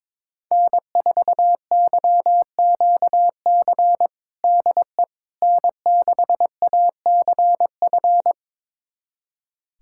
Morse code, heard as a keyed tone, about 22 wpm, 700 Hz. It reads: N4YQC DE N6ACF
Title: USA Callsign DE Callsign - 5 Characters - 22wpm